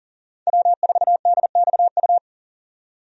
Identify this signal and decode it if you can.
Morse code, heard as a keyed tone, about 40 wpm, 700 Hz. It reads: W4DXU
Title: USA Call Signs - 5 Characters - 40wpm